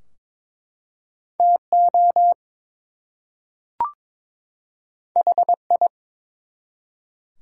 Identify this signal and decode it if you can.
Morse code, heard as a keyed tone, about 22 wpm, 700 Hz. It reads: TO HI
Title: Two Letter Words - 22wpm